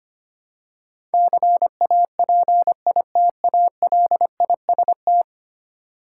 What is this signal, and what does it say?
Morse code, heard as a keyed tone, about 25 wpm, 700 Hz. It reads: CAPITALIST
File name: Ten Letter Words - 25wpm